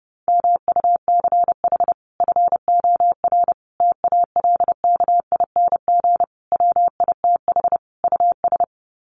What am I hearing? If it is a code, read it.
Morse code, heard as a keyed tone, about 29 wpm, 700 Hz. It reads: MUCH FOR TALKING WITH US
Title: Sets of 5 Words Encore 30wpm